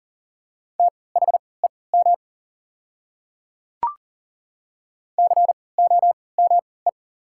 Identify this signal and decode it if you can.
Morse code, heard as a keyed tone, about 40 wpm, spaced as for 23 wpm, 690 Hz. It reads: THEM COME
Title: 3x Character Spacing - Top 200 Words - 40wpm